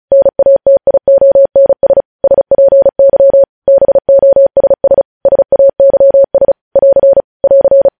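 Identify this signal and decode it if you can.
Morse code, heard as a keyed tone, about 36 wpm, 550 Hz. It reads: NATIONS SPY BOSS SAYS <AR> <AR>
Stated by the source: Morse code news headlines sent at 35 wpm.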